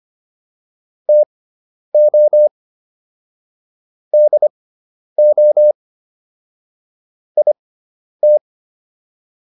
Morse code, heard as a keyed tone, about 25 wpm, spaced as for 10 wpm, 600 Hz. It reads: TO DO IT